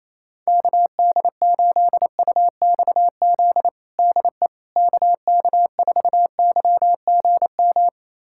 Morse code, heard as a keyed tone, about 28 wpm, 700 Hz. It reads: KD8UXZ DE KK4YGM
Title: USA Callsign DE Callsign - 6 Characters - 28wpm